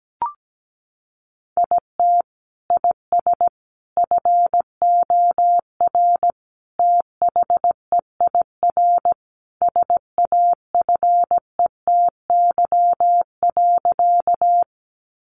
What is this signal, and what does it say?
Morse code, heard as a keyed tone, about 17 wpm, 700 Hz. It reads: IT IS FOR THEIR SAFETY.
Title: Sentences from Top 700 Words 17wpm